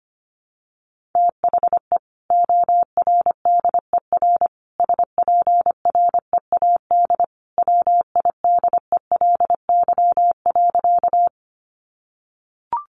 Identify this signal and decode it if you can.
Morse code, heard as a keyed tone, about 25 wpm, 700 Hz. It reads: THE ORDER SPREAD WIDELY.
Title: Sentences from Top 1200 Words 25wpm